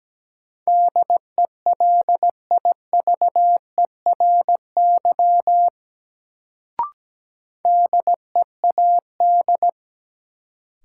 Morse code, heard as a keyed tone, about 17 wpm, 700 Hz. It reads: DELIVERY DEAD
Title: Sentences from Top 1500 Words - New Word Drill - 17wpm